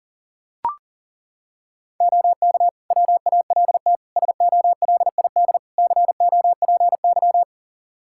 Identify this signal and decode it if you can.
Morse code, heard as a keyed tone, about 40 wpm, 700 Hz. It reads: OK WALT SOLID COPY
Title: Practice QSO - Exchange 3 - More Info - 40wpm